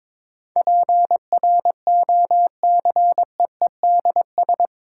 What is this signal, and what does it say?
Morse code, heard as a keyed tone, about 22 wpm, 700 Hz. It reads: PROCEEDS
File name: practice / 22wpm / Eight Letter Words - 22wpm